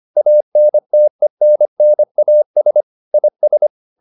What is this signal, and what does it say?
Morse code, heard as a keyed tone, about 25 wpm, 600 Hz. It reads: ANTENNAS IS